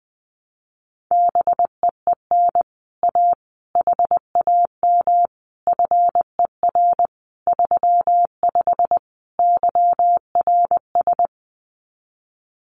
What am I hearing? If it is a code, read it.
Morse code, heard as a keyed tone, about 20 wpm, 700 Hz. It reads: BEEN A HAM FER 35 YRS
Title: Common QSO Phrases Encore 20wpm